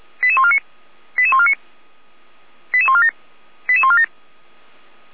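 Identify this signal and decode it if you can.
Five-tone selective call (ZVEI), heard as quick runs of five tones, tones of about 70 ms. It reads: 88149 88149 88147 88147
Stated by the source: ZVEI five-tone call received via SDR.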